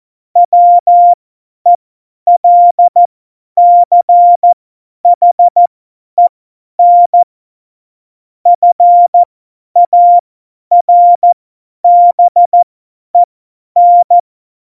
Morse code, heard as a keyed tone, about 14 wpm, spaced as for 10 wpm, 700 Hz. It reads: WELCHEN FARBEN
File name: Morsekurs / Witze / Witz 10. September 2019, 50 F70